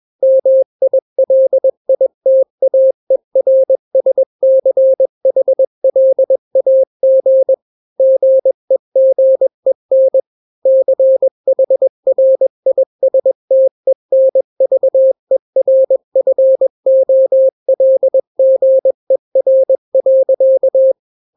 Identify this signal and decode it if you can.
Morse code, heard as a keyed tone, about 21 wpm, 530 Hz. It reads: MILITAERSCHLAG GEGEN CHRISTENVERFOLGER.
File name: daily / 20wpm